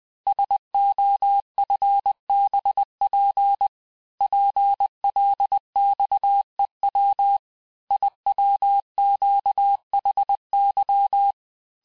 Morse code, heard as a keyed tone, about 20 wpm, 800 Hz. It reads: SOFBP PLXEW IWQHY
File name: cours-858-texte-chiffre-vitesse-19mpm